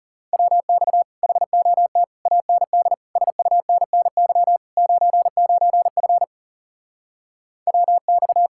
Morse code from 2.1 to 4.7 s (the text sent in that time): AND SUNNY